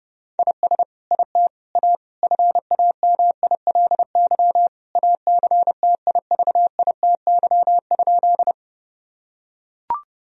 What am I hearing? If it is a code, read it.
Morse code, heard as a keyed tone, about 30 wpm, 700 Hz. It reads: IS IT A FAMILY ACTIVITY?
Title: Sentences from Top 1000 Words - Review Part 4 - 30wpm